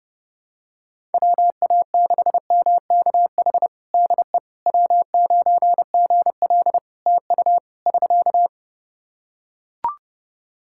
Morse code, heard as a keyed tone, about 30 wpm, 700 Hz. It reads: WA6MKH DE W9GL TU <SK>
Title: Practice QSO - Exchange 4 - Ending - 30wpm